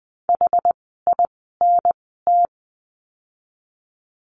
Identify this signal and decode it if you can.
Morse code, heard as a keyed tone, about 20 wpm, spaced as for 14 wpm, 700 Hz. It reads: HINT
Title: 061 Lesson 11f - Letter h - Words 20wpm